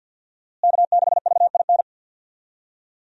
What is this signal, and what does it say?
Morse code, heard as a keyed tone, about 47 wpm, 700 Hz. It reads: K6VIN
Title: USA Call Signs - 5 Characters - Repeated 3x - 50wpm